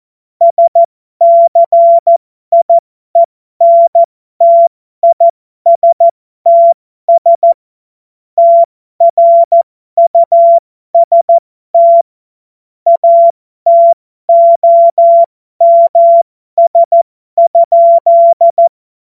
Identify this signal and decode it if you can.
Morse code, heard as a keyed tone, about 14 wpm, spaced as for 12 wpm, 680 Hz. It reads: SCIENTISTS TRUST ATOMS?
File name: Morsekurs / Witze